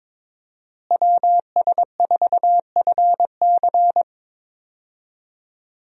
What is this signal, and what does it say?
Morse code, heard as a keyed tone, about 22 wpm, 700 Hz. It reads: WS4FC
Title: USA Call Signs - 5 Characters - Repeated 3x - 22wpm